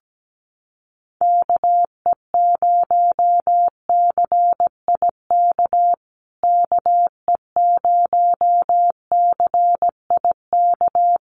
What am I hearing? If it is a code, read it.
Morse code, heard as a keyed tone, about 17 wpm, 700 Hz. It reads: KE0CIK KE0CIK
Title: USA Call Signs Repeated 3x 17wpm